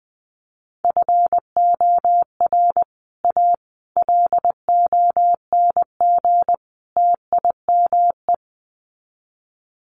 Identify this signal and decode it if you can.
Morse code, heard as a keyed tone, about 20 wpm, 700 Hz. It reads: FOR A LONG TIME
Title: Sets of 4 Words 20wpm